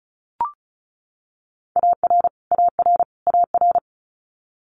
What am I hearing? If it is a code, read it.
Morse code, heard as a keyed tone, about 36 wpm, 700 Hz. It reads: AR AR AR
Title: Multi-letter Phonemes Repeated 3x 35wpm